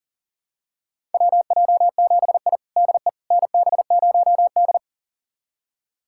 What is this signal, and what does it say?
Morse code, heard as a keyed tone, about 40 wpm, 700 Hz. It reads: WJ7I DE NB0D
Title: USA Callsign DE Callsign - 4 Characters - 40wpm